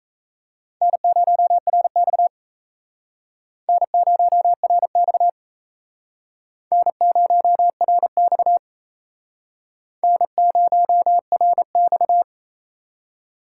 Morse code, heard as a keyed tone, about 37 wpm, 700 Hz. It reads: N0RX N0RX N0RX N0RX